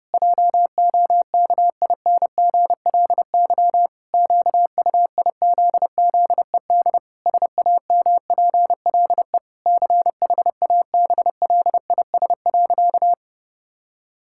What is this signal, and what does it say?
Morse code, heard as a keyed tone, about 32 wpm, 700 Hz. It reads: JOKINGLY QUIZZED SAMPLE CHABLIS.
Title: Pangram Challenge v2 - 30wpm